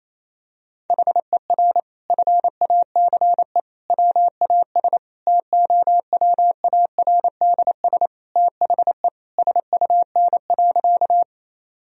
Morse code, heard as a keyed tone, about 28 wpm, 700 Hz. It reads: HER FACE WAS TOWARDS THE SUN.
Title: Sentences from Top 1000 Words - Review Part 2 - 28wpm